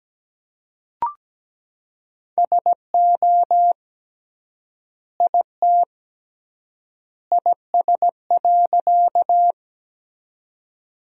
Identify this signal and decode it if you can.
Morse code, heard as a keyed tone, about 17 wpm, 700 Hz. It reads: SO IT IS.
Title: 3x Word-Spacing - Sentences from Top 100 Words - 17wpm